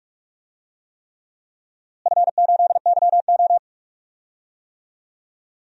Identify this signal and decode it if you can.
Morse code, heard as a keyed tone, about 46 wpm, 700 Hz. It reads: F8YO
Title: DXCC-100 Call Signs - 4 Characters - Repeated 3x - 45wpm